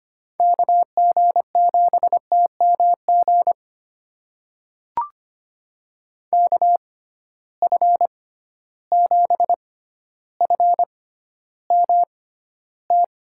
Morse code, heard as a keyed tone, about 25 wpm, spaced as for 13 wpm, 700 Hz. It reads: KG7TMG KF7FMT
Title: USA Call Signs - 6x Character Spacing - 25wpm